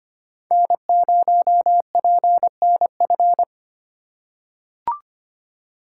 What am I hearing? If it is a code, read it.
Morse code, heard as a keyed tone, about 25 wpm, 700 Hz. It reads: N0PNF
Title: USA Call Signs - 2x Character Spacing - 25wpm